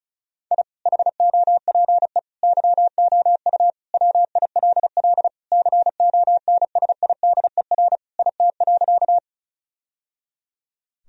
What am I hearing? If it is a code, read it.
Morse code, heard as a keyed tone, about 34 wpm, 700 Hz. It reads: I HOPE YOU WILL CONSIDER IT.
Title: Sentences from Top 1000 Words - Review Part 4 - 35wpm